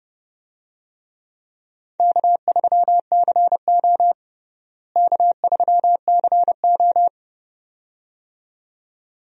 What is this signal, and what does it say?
Morse code, heard as a keyed tone, about 30 wpm, 700 Hz. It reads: K3CO K3CO
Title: USA Call Signs - 4 Characters - Repeated 2x - 30wpm